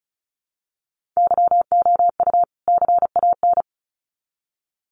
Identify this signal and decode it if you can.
Morse code, heard as a keyed tone, about 34 wpm, 700 Hz. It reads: YOU CAN
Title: Sets of 2 Words 35wpm